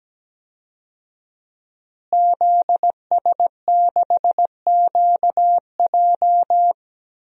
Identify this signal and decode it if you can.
Morse code, heard as a keyed tone, about 17 wpm, 700 Hz. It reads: ZS6QJ
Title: DXCC-100 Call Signs - 5 Characters - Repeated 2x - 17wpm